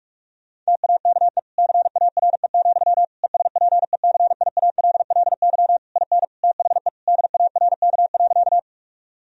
Morse code, heard as a keyed tone, about 46 wpm, 700 Hz. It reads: TAKE CARE, ESPECIALLY IN THE DARK.
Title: Sentences from Top 1000 Words - Review Part 4 - 45wpm